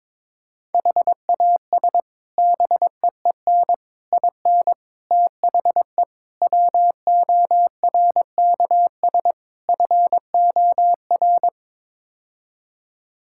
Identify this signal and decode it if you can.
Morse code, heard as a keyed tone, about 22 wpm, 700 Hz. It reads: HAS BEEN IN THE WORKS FOR
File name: practice / 22wpm / Sets of 6 Words - 22wpm